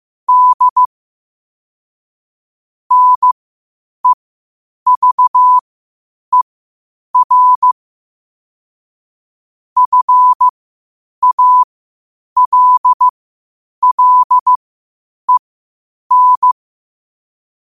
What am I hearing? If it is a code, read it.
Morse code, heard as a keyed tone, about 15 wpm, spaced as for 8 wpm, 1000 Hz. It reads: D NEVER FALLEN
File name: qotdfiles / 8wpm